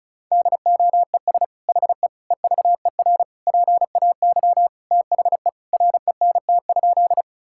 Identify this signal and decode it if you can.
Morse code, heard as a keyed tone, about 34 wpm, 700 Hz. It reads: DOES HE EVER PAY THE RENT?
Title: Sentences from Top 1000 Words - Review Part 5 - 35wpm